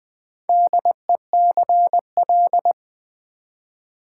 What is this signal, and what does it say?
Morse code, heard as a keyed tone, about 20 wpm, 700 Hz. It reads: DECL
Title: Build A Word - 4 Letters At A Time - 11-Letter Words - 20wpm